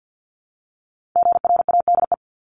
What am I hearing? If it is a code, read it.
Morse code, heard as a keyed tone, about 50 wpm, 700 Hz. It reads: GRADE
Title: Multi-letter Phonemes 50wpm